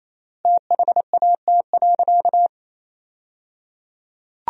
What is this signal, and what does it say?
Morse code, heard as a keyed tone, about 28 wpm, 700 Hz. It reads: THAT.
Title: 3x Word-Spacing - Sentences from Top 300 Words - 28wpm